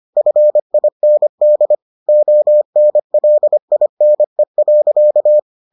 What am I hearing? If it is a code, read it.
Morse code, heard as a keyed tone, about 25 wpm, 600 Hz. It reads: FIND ONLINE.